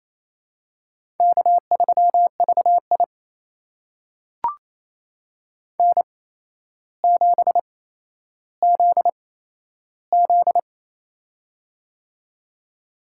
Morse code, heard as a keyed tone, about 28 wpm, spaced as for 12 wpm, 700 Hz. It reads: K3VI N7ZZ
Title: USA Call Signs - 4 Characters - 8x Character Spacing - 28wpm